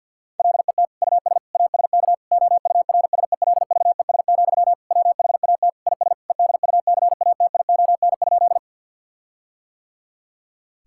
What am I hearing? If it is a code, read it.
Morse code, heard as a keyed tone, about 50 wpm, 700 Hz. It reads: LET US ASK OURSELVES, WHAT IS EDUCATION?
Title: Sentences from Top 1600 Words 50wpm